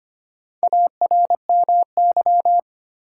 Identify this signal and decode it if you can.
Morse code, heard as a keyed tone, about 25 wpm, 700 Hz. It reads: ARMY
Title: Sentences from Top 800 Words - New Word Drill - 25wpm